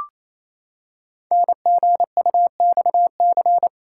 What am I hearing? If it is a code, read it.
Morse code, heard as a keyed tone, about 29 wpm, 700 Hz. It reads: NGUXC
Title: Code Groups - 5 Letter:Numbers - 28wpm